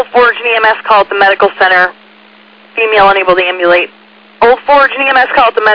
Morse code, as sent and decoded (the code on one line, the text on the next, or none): none